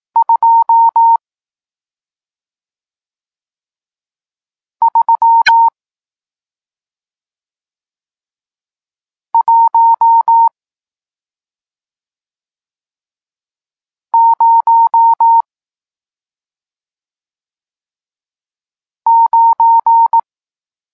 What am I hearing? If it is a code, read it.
Morse code, heard as a keyed tone, about 18 wpm, 920 Hz. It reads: 2 3 1 0 9
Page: Tekst 40 – Nauka znaków przestankowych